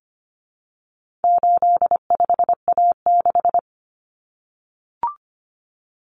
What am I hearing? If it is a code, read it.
Morse code, heard as a keyed tone, about 25 wpm, 700 Hz. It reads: 85A6
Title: Common Cut Numbers - 4-Digit 25wpm